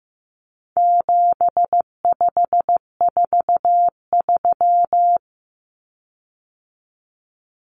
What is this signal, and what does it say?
Morse code, heard as a keyed tone, about 15 wpm, 700 Hz. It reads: 7543
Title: Common Cut Numbers - 4-Digit 15wpm